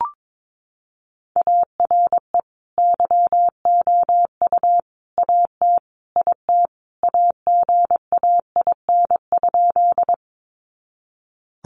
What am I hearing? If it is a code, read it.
Morse code, heard as a keyed tone, about 22 wpm, 700 Hz. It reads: ARE YOU AT IT AGAIN?
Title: Sentences from Top 200 Words 22wpm